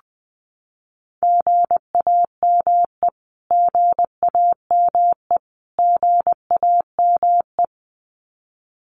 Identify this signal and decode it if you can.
Morse code, heard as a keyed tone, about 20 wpm, 700 Hz. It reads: GAME GAME GAME
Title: Top 500 Words Repeated 20wpm